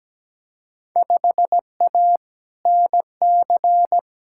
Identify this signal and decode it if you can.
Morse code, heard as a keyed tone, about 17 wpm, 700 Hz. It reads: 5A NC
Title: Field Day Exchanges - 17wpm